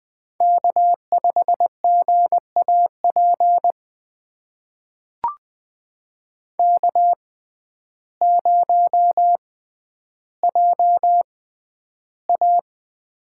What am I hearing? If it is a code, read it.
Morse code, heard as a keyed tone, about 20 wpm, spaced as for 11 wpm, 700 Hz. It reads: K5GAP K0JA
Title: USA Call Signs - 6x Character Spacing - 20wpm